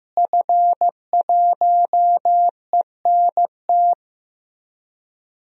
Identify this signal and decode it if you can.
Morse code, heard as a keyed tone, about 16 wpm, 700 Hz. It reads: F1ENT
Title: DXCC-100 Call Signs - 5 Characters - Repeated 2x - 15wpm